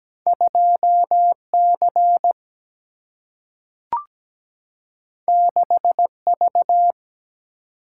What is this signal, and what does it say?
Morse code, heard as a keyed tone, about 17 wpm, 700 Hz. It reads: 2C 6V
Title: Single-Letter Single-Number 17wpm